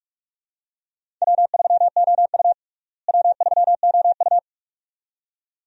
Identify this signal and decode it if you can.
Morse code, heard as a keyed tone, about 46 wpm, 700 Hz. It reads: W3OU W3OU
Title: USA Call Signs - 4 Characters - Repeated 2x - 45wpm